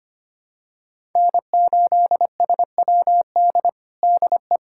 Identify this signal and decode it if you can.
Morse code, heard as a keyed tone, about 25 wpm, 700 Hz. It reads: N8SWD DE
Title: USA Callsign DE Callsign 25wpm